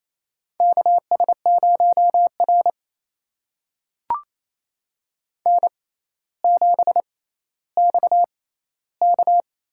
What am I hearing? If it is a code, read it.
Morse code, heard as a keyed tone, about 28 wpm, spaced as for 14 wpm, 700 Hz. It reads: KS0R N7XK
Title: USA Call Signs - 6x Character Spacing - 28wpm